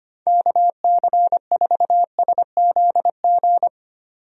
Morse code, heard as a keyed tone, about 25 wpm, 700 Hz. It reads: KC4SZG
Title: USA Call Signs - 6 Characters - 25wpm